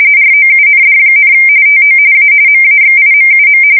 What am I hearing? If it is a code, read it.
RTTY, about 45.45 baud. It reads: ROUND, TO A MAXIMUM
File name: fsk170 / battleshipnj